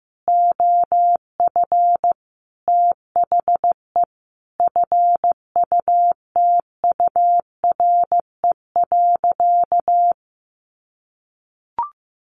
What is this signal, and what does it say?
Morse code, heard as a keyed tone, about 15 wpm, 700 Hz. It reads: OF THE FUTURE.
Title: Sentences from Top 1500 Words 15wpm